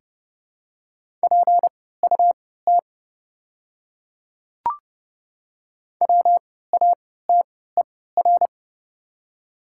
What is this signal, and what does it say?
Morse code, heard as a keyed tone, about 31 wpm, spaced as for 17 wpm, 690 Hz. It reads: PUT WATER
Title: 3x Character Spacing - Top 200 Words - 30wpm